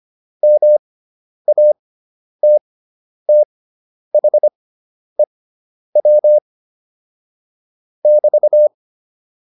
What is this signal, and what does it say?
Morse code, heard as a keyed tone, about 25 wpm, spaced as for 10 wpm, 600 Hz. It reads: MATTHEW <BT>